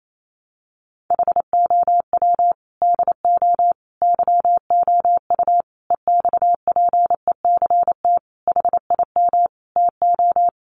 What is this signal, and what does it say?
Morse code, heard as a keyed tone, about 28 wpm, 700 Hz. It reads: HOW DO YOU EXPECT HIM TO
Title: Sentences from Top 1100 Words 28wpm